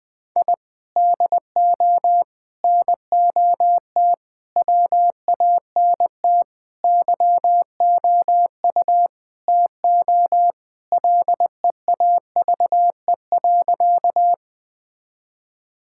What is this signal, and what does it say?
Morse code, heard as a keyed tone, about 20 wpm, 700 Hz. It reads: I DO NOT WANT YOU TO LEAVE.
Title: Sentences with 5-Word Phrases - 20wpm